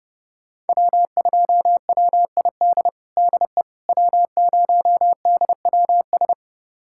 Morse code, heard as a keyed tone, about 32 wpm, 700 Hz. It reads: W2WID DE W0DWS
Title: USA Callsign DE Callsign - 5 Characters - 30wpm